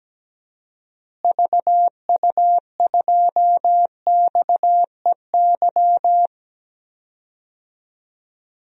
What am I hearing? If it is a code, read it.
Morse code, heard as a keyed tone, about 17 wpm, 700 Hz. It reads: VU2XEY
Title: DXCC-100 Call Signs - 6 Characters - Repeated 2x - 17wpm